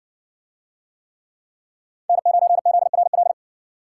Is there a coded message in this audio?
Yes: Morse code from 2.1 to 3.3 s, heard as a keyed tone, about 60 wpm, 700 Hz. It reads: N9ZRL